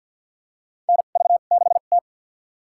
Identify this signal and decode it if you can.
Morse code, heard as a keyed tone, about 53 wpm, spaced as for 37 wpm, 700 Hz. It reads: NV6T
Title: USA Call Signs - 4 Characters - 2x Character Spacing - 50wpm